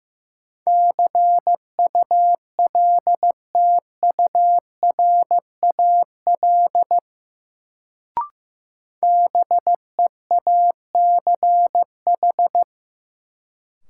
Morse code, heard as a keyed tone, about 15 wpm, 700 Hz. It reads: CULTURAL BEACH
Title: Sentences from Top 1500 Words - New Word Drill Alt - 15wpm